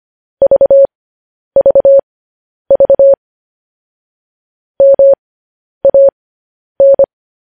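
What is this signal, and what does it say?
Morse code, heard as a keyed tone, about 25 wpm, spaced as for 10 wpm, 550 Hz. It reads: VVV MAN
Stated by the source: Morse code news headlines sent at 10 wpm.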